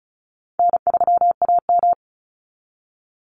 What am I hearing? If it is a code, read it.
Morse code, heard as a keyed tone, about 36 wpm, 700 Hz. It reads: N3AM
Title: Call Signs - 35wpm